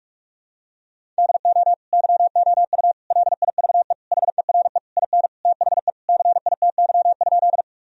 Morse code, heard as a keyed tone, about 46 wpm, 700 Hz. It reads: DO YOU LIVE HERE IN THE CITY?
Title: Sentences from Top 1000 Words - Review Part 1 - 45wpm